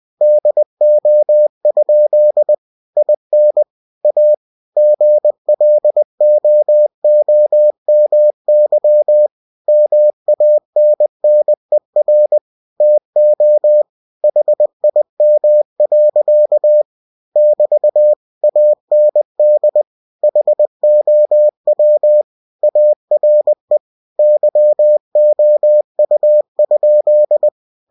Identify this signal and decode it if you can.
Morse code, heard as a keyed tone, about 20 wpm, 600 Hz. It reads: DO? IN A GLOOMY MANNER TO HIM. <BT> AND HOW ARE YOU?
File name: cw-books / winnie-the-pooh / winnie-20wpm-page-0026